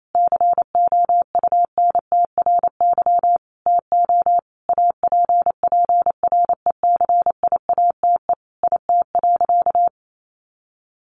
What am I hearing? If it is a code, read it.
Morse code, heard as a keyed tone, about 28 wpm, 700 Hz. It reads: COUNTRY TO APPRECIATE IT.
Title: Sentences from Top 1300 Words 28wpm